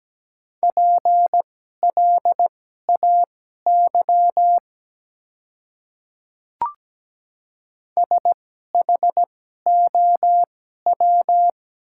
Morse code, heard as a keyed tone, about 17 wpm, spaced as for 12 wpm, 700 Hz. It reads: PLAY SHOW
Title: 2x Character Spacing - Top 200 Words - 17wpm